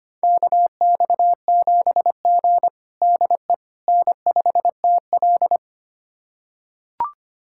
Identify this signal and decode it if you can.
Morse code, heard as a keyed tone, about 25 wpm, 700 Hz. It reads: KX7G DE N5TL
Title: USA Callsign DE Callsign - 4 Characters - 25wpm